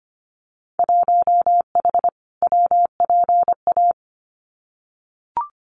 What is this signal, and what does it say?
Morse code, heard as a keyed tone, about 25 wpm, 700 Hz. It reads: 1H WPA
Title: Winter Field Day Exchanges 25wpm